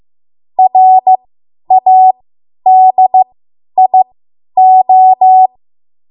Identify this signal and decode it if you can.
Morse code, heard as a keyed tone, about 15 wpm, spaced as for 10 wpm, 750 Hz. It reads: RADIO
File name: Morse / 10WPM